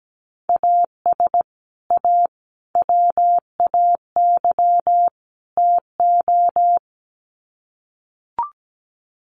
Morse code, heard as a keyed tone, about 17 wpm, 700 Hz. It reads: AS A WAY TO
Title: Sets of 4 Words 17wpm